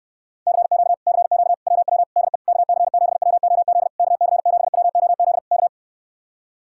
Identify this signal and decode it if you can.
Morse code, heard as a keyed tone, about 68 wpm, 700 Hz. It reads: CQ CQ WW DE KC7WOZ KC7WOZ K